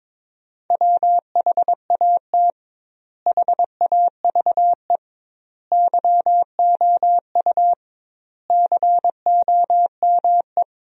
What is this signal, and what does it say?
Morse code, heard as a keyed tone, about 22 wpm, 700 Hz. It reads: WHAT HAVE YOU COME
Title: 2x Word-Spacing - Sentences from Top 100 Words - 22wpm